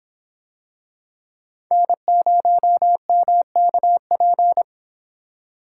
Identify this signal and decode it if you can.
Morse code, heard as a keyed tone, about 26 wpm, 700 Hz. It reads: N0MKP